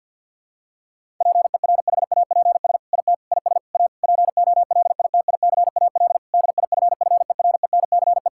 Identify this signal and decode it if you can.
Morse code, heard as a keyed tone, about 52 wpm, 700 Hz. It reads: PERHAPS IT IS A POLITICAL DIFFERENCE
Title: Sentences from Top 1000 Words - Review Part 5 - 50wpm